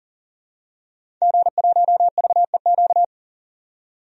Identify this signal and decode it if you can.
Morse code, heard as a keyed tone, about 39 wpm, 700 Hz. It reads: G1VEQ
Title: DXCC-100 Call Signs - 5 Characters - 40wpm